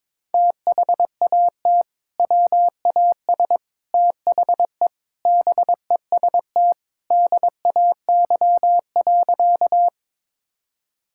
Morse code, heard as a keyed tone, about 22 wpm, 700 Hz. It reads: THAT WAS THE BEST DAY.
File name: practice / 22wpm / Sentences with 3-Word Phrases - 22wpm